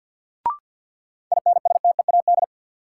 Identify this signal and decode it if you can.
Morse code, heard as a keyed tone, about 51 wpm, 700 Hz. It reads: INSTEAD
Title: Sentences from Top 800 Words - New Word Drill - 50wpm